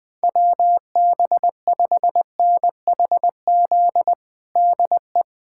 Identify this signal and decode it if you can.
Morse code, heard as a keyed tone, about 20 wpm, 700 Hz. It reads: WB5NHZ DE
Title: USA Callsign DE Callsign 20wpm